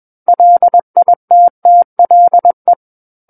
Morse code, heard as a keyed tone, about 21 wpm, 700 Hz. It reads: LITTLE
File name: Book One / 12-Chapter 12 - 21 WPM